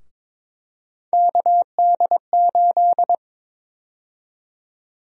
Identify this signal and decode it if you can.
Morse code, heard as a keyed tone, about 22 wpm, 700 Hz. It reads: KD8
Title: USA Call Sign Prefixes - Repeated 2x - 22wpm